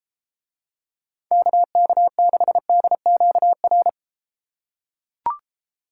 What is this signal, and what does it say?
Morse code, heard as a keyed tone, about 33 wpm, 700 Hz. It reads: KK6DQR